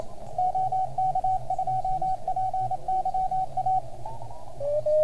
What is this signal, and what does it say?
Morse code, heard as a keyed tone, about 27 wpm, 750 Hz, with radio noise. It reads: OK2PYA